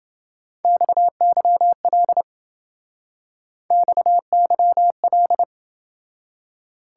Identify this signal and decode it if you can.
Morse code, heard as a keyed tone, about 28 wpm, 700 Hz. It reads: XYL XYL